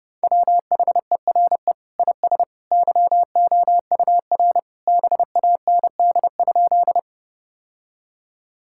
Morse code, heard as a keyed tone, about 30 wpm, 700 Hz. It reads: WHERE IS YOUR BAND?
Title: Sentences from Top 1600 Words 30wpm